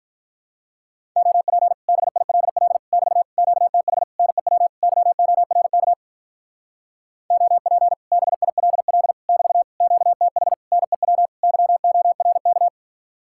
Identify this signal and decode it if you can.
Morse code, heard as a keyed tone, about 43 wpm, 700 Hz. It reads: OP BILL <BT> QTH NEW YORK OP BILL <BT> QTH NEW YORK
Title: Speed-Racing Common QSO Phrases - 35wpm